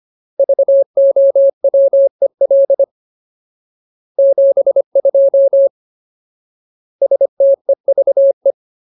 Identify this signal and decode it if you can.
Morse code, heard as a keyed tone, about 25 wpm, 550 Hz. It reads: VOWEL 72 STEVE
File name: Ultimate Word List / 25WPM Ultimate